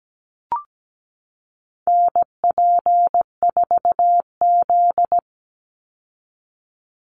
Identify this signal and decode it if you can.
Morse code, heard as a keyed tone, about 17 wpm, 700 Hz. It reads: NP4Z
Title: Call Signs Repeated 2x - 17wpm